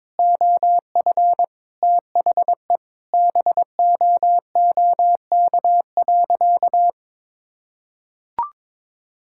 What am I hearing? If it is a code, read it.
Morse code, heard as a keyed tone, about 21 wpm, 700 Hz. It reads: OF THE BOOK.
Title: Sentences from Top 1000 Words - Review Part 5 - 22wpm